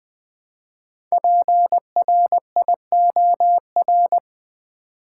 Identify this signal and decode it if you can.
Morse code, heard as a keyed tone, about 20 wpm, 700 Hz. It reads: PRIOR
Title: Build A Word - 1 Letter At A Time - 5-Letter Words - 20wpm